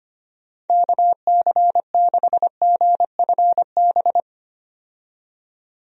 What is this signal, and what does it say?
Morse code, heard as a keyed tone, about 25 wpm, 700 Hz. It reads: KC6GFB
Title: USA Call Signs - 6 Characters - 25wpm